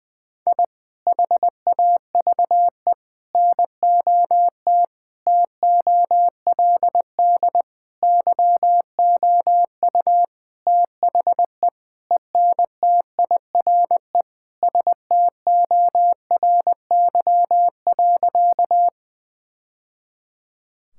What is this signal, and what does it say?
Morse code, heard as a keyed tone, about 21 wpm, 700 Hz. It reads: I HAVE NOT TOLD YOU THE ENTIRE STORY.
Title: Sentences from Top 1000 Words - Review Part 5 - 20wpm